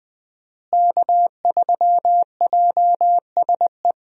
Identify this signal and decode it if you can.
Morse code, heard as a keyed tone, about 20 wpm, 700 Hz. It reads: K3JSE
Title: USA Call Signs - 5 Characters - Repeated 3x - 20wpm